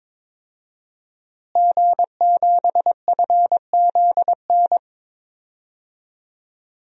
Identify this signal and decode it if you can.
Morse code, heard as a keyed tone, about 22 wpm, 700 Hz. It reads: G7FZN
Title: DXCC-100 Call Signs - 5 Characters - Repeated 2x - 22wpm